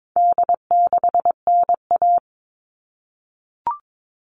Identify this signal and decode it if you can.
Morse code, heard as a keyed tone, about 22 wpm, 700 Hz. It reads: D6NA
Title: All Cut Numbers - 4-Digit 22wpm